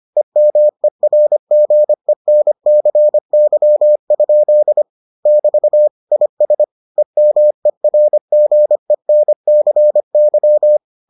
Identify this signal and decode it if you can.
Morse code, heard as a keyed tone, about 25 wpm, 600 Hz. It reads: EMERGENCY? <BT> IS EMERGENCY